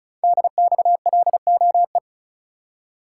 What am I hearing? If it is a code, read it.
Morse code, heard as a keyed tone, about 36 wpm, 690 Hz. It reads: DXLOE
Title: Code Groups - 5 Letters - 35wpm